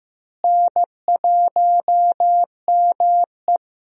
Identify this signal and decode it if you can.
Morse code, heard as a keyed tone, about 15 wpm, 700 Hz. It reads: N1ME
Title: USA Call Signs - 4 Characters - Repeated 3x - 15wpm